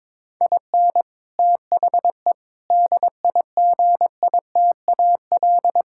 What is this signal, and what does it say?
Morse code, heard as a keyed tone, about 22 wpm, 700 Hz. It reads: IN THE DIGITAL
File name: Difficult Sentences with Hints - 22wpm